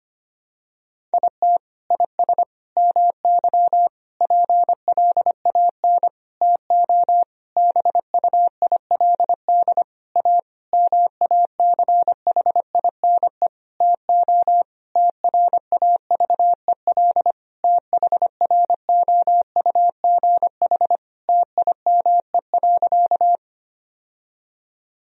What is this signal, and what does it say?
Morse code, heard as a keyed tone, about 25 wpm, 700 Hz. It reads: IT IS MY PLAN TO BUILD A MACHINE TO TRAVEL THROUGH TIME.
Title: Sentences from Top 1000 Words - Review Part 2 - 25wpm